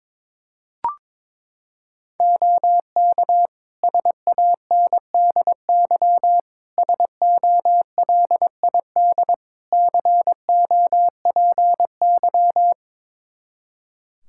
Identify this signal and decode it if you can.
Morse code, heard as a keyed tone, about 22 wpm, 700 Hz. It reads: OK SANDY SOLID COPY
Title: K1USN SST QSO Practice 22wpm